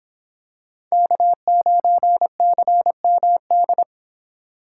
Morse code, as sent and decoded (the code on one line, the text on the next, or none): -.- ----. -.-. -- -..
K9CMD